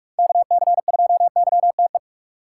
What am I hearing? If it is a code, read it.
Morse code, heard as a keyed tone, about 46 wpm, 700 Hz. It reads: KC2YTE
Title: USA Call Signs - 3x Character Spacing - 45wpm